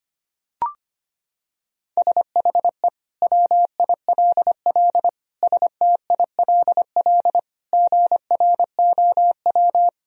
Morse code, heard as a keyed tone, about 25 wpm, 700 Hz. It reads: SHE WILL STILL GROW
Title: Sentences from Top 1000 Words - Review Part 1 - 25wpm